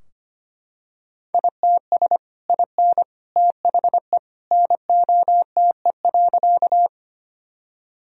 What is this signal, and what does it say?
Morse code, heard as a keyed tone, about 25 wpm, 700 Hz. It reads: ITS IN THE NOTE.
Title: Sentences from Top 1000 Words - Review Part 2 - 25wpm